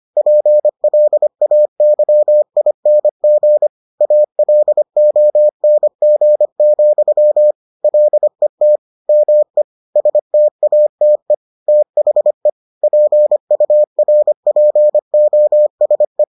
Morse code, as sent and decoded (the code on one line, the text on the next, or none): .--. .-.. .- -.-- .. -. --. / .- .-.. --- -. --. --..-- / .-.. - / -- . / ... - .- - . / - .... / .--. ..- .-. .--. --- ... .
PLAYING ALONG, LET ME STATE THE PURPOSE